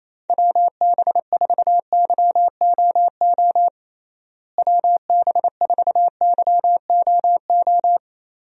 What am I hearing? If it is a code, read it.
Morse code, heard as a keyed tone, about 28 wpm, 700 Hz. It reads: WB4YOO WB4YOO
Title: USA Call Signs - 6 Characters - Repeated 3x - 28wpm